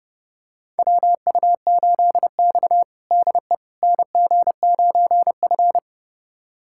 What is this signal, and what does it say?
Morse code, heard as a keyed tone, about 30 wpm, 700 Hz. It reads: WU8X DE NG9F
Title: USA Callsign DE Callsign - 4 Characters - 30wpm